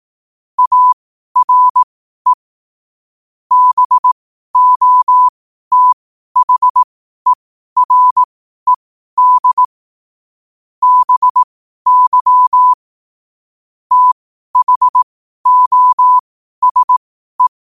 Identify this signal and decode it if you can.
Morse code, heard as a keyed tone, about 18 wpm, spaced as for 12 wpm, 1000 Hz. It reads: ARE BOTHERED BY THOSE
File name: qotdfiles / 12wpm